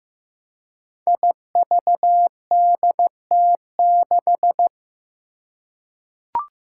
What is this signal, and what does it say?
Morse code, heard as a keyed tone, about 15 wpm, 700 Hz. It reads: IVDT6
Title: Five Challenging Letter-Number Combinations 15wpm